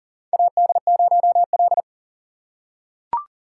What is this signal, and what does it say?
Morse code, heard as a keyed tone, about 37 wpm, 700 Hz. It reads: AD0L
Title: USA Call Signs - 4 Characters - 2x Character Spacing - 40wpm